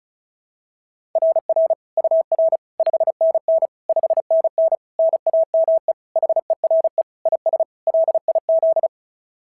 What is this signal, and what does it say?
Morse code, heard as a keyed tone, about 36 wpm, 690 Hz. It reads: RR UR 5NN 5NN NAME HERE IS LIZ
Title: SOTA QSO Practice 35wpm